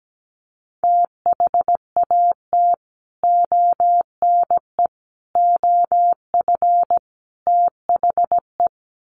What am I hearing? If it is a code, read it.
Morse code, heard as a keyed tone, about 17 wpm, 700 Hz. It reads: THAT ONE OF THE
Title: Sets of 4 Words 17wpm